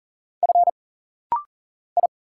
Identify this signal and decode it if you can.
Morse code, heard as a keyed tone, about 40 wpm, 700 Hz. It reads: F I
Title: Single Letters - Rapid-Fire 40wpm